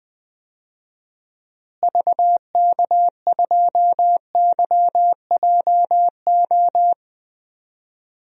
Morse code, heard as a keyed tone, about 20 wpm, 700 Hz. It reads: VK2YJO
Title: DXCC-100 Call Signs - 6 Characters - Repeated 2x - 20wpm